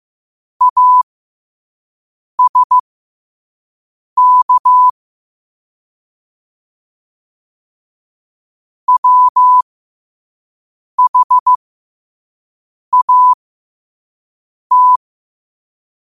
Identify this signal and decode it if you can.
Morse code, heard as a keyed tone, about 15 wpm, spaced as for 5 wpm, 1000 Hz. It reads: ASK WHAT
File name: qotdfiles / 5wpm